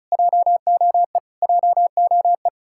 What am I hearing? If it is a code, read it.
Morse code, heard as a keyed tone, about 46 wpm, 700 Hz. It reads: JOE JOE